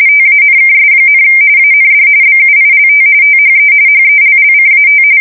RTTY, about 45.45 baud. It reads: TANKS, AND BUILT THE HELLCAT
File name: fsk170 / usnavy4